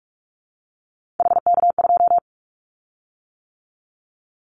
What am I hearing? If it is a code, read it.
Morse code, heard as a keyed tone, about 44 wpm, 700 Hz. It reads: HK2
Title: Callsign Prefixes for Top 100 DXCC - 45wpm